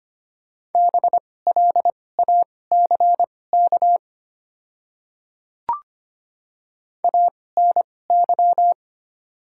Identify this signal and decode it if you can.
Morse code, heard as a keyed tone, about 25 wpm, spaced as for 18 wpm, 700 Hz. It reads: BLACK ANY
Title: 2x Character Spacing - Top 500 Words - 25wpm